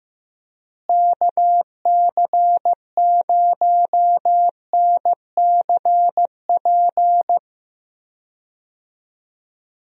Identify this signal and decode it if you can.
Morse code, heard as a keyed tone, about 15 wpm, 700 Hz. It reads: KC0NCP
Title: USA Call Signs - 6 Characters - Repeated 3x - 15wpm